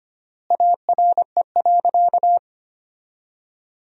Morse code, heard as a keyed tone, about 25 wpm, 700 Hz. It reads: ARE.
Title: 3x Word-Spacing - Sentences from Top 100 Words - 25wpm